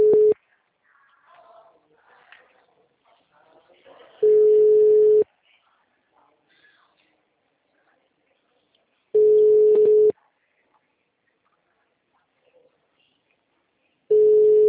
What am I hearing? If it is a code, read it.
Ringback tone (Europe, 425 Hz).